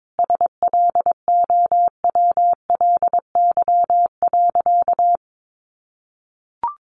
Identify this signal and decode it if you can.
Morse code, heard as a keyed tone, about 22 wpm, 700 Hz. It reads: SLOWLY.
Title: Sentences from Top 1100 Words 22wpm